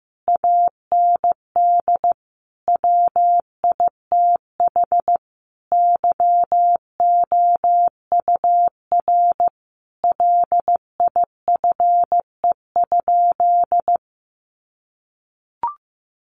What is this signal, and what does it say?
Morse code, heard as a keyed tone, about 15 wpm, 700 Hz. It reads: AND WITH YOUR LIFE?
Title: Sentences from Top 900 Words 15wpm